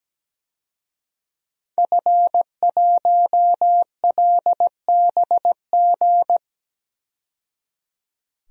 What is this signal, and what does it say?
Morse code, heard as a keyed tone, about 17 wpm, 700 Hz. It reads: F1LBG
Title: DXCC-100 Call Signs - 5 Characters - 17wpm